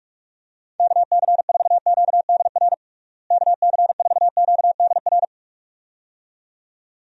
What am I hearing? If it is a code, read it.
Morse code, heard as a keyed tone, about 46 wpm, 700 Hz. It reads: KC4QDR KC4QDR
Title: USA Call Signs - 6 Characters - Repeated 2x - 45wpm